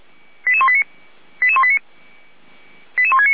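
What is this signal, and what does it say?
Five-tone selective call (ZVEI), heard as quick runs of five tones, tones of about 70 ms. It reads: 88189 88189 88169
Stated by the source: ZVEI five-tone selective call 88191 received by SDR.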